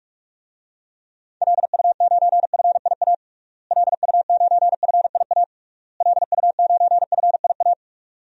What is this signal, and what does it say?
Morse code, heard as a keyed tone, about 46 wpm, 700 Hz. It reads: LU9FIA LU9FIA LU9FIA
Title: DXCC-100 Call Signs - 6 Characters - Repeated 3x - 45wpm